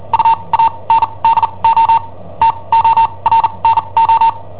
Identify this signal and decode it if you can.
Morse code, heard as a keyed tone, about 40 wpm, spaced as for 27 wpm, 910 Hz. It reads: UANDO TORNO
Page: Cassetta esercizi preparata da I7OHP per IW5ELP